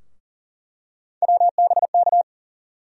Morse code, heard as a keyed tone, about 40 wpm, 700 Hz. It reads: WBK
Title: USA Call Sign Postfixes - 3 Letters 40wpm